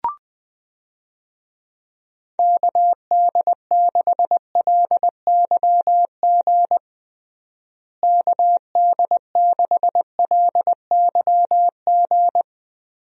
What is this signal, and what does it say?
Morse code, heard as a keyed tone, about 20 wpm, 700 Hz. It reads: KD6LYG KD6LYG
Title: USA Call Signs - 6 Characters - Repeated 2x - 20wpm